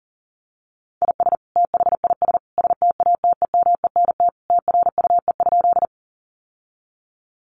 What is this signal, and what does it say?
Morse code, heard as a keyed tone, about 40 wpm, 700 Hz. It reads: IS THIS STATEMENT TRUE?
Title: Sentences from Top 1200 Words 40wpm